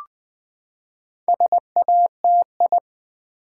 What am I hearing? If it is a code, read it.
Morse code, heard as a keyed tone, about 20 wpm, 700 Hz. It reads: SATI
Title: Build A Word - 4 Letters At A Time - 12-Letter Words - 20wpm